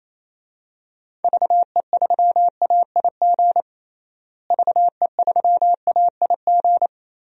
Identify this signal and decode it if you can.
Morse code, heard as a keyed tone, about 28 wpm, 700 Hz. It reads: VE3AIG VE3AIG
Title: DXCC-100 Call Signs - 6 Characters - Repeated 3x - 28wpm